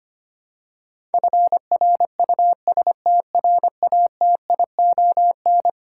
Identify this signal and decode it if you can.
Morse code, heard as a keyed tone, about 25 wpm, 700 Hz. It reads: FRUSTRATION
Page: morse-code-ninja-practice-25wpm